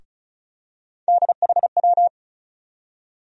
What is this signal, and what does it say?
Morse code, heard as a keyed tone, about 34 wpm, 700 Hz. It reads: DHW
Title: USA Call Sign Postfixes - 3 Letters 35wpm